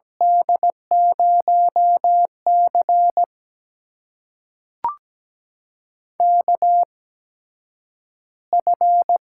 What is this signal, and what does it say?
Morse code, heard as a keyed tone, about 17 wpm, spaced as for 9 wpm, 700 Hz. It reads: D0C KF
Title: USA Call Signs - 4 Characters - 8x Character Spacing - 17wpm